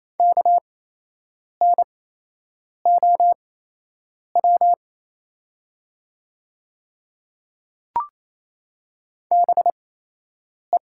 Morse code, heard as a keyed tone, about 27 wpm, spaced as for 8 wpm, 690 Hz. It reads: KNOW BE
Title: 8x Character Spacing - Top 300 Words - 28wpm